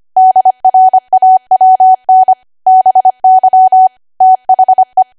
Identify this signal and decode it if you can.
Morse code, heard as a keyed tone, about 25 wpm, 750 Hz. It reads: DRAWN BY THE
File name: Archive / 25 WPM / 25WPM